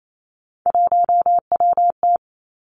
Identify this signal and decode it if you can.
Morse code, heard as a keyed tone, about 28 wpm, 700 Hz. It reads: 1WT
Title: Three Challenging Letter-Number Combinations 28wpm